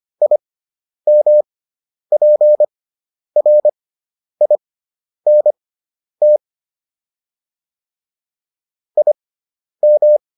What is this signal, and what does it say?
Morse code, heard as a keyed tone, about 25 wpm, spaced as for 10 wpm, 600 Hz. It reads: IMPRINT IM